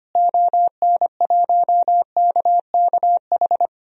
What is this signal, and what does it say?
Morse code, heard as a keyed tone, about 25 wpm, 700 Hz. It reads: ON1KKH
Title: DXCC-100 Call Signs - 6 Characters - Repeated 2x - 25wpm